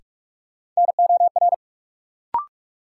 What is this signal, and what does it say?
Morse code, heard as a keyed tone, about 46 wpm, 700 Hz. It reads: NOR
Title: Sentences from Top 1500 Words - New Word Drill - 45wpm